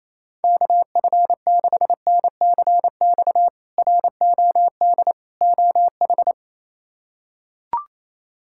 Morse code, heard as a keyed tone, about 27 wpm, 700 Hz. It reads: KF6NCX ROD OH
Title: K1USN SST QSO Practice - 28wpm